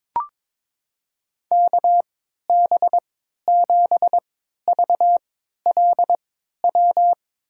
Morse code, heard as a keyed tone, about 22 wpm, spaced as for 13 wpm, 700 Hz. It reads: KB7VLW
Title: USA Call Signs - 6 Characters - 3x Character Spacing - 22wpm